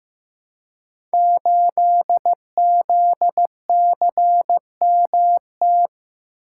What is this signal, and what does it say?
Morse code, heard as a keyed tone, about 15 wpm, 700 Hz. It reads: 8ZCMT
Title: Code Groups - 5 Letter:Numbers - 15wpm